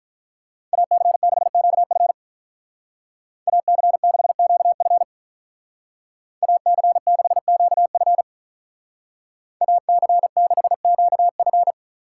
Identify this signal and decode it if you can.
Morse code, heard as a keyed tone, about 44 wpm, 700 Hz. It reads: AC6QF AC6QF AC6QF AC6QF